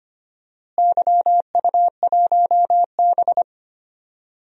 Morse code, heard as a keyed tone, about 25 wpm, 700 Hz. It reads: YU1B
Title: DXCC-100 Call Signs - 4 Characters - Repeated 3x - 25wpm